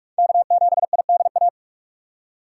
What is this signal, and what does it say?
Morse code, heard as a keyed tone, about 47 wpm, 690 Hz. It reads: K7IDA
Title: USA Call Signs - 5 Characters - 2x Character Spacing - 45wpm